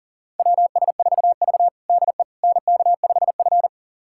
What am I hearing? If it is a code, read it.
Morse code, heard as a keyed tone, about 40 wpm, 700 Hz. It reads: WS4V DE NK5F
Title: USA Callsign DE Callsign - 4 Characters - 40wpm